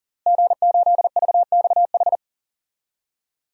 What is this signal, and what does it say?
Morse code, heard as a keyed tone, about 40 wpm, 700 Hz. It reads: G8VXH
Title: DXCC-100 Call Signs - 5 Characters - Repeated 2x - 40wpm